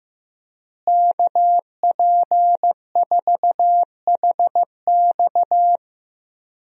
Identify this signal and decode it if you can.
Morse code, heard as a keyed tone, about 16 wpm, 700 Hz. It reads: KP4HX
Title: USA Call Signs - 5 Characters - Repeated 3x - 15wpm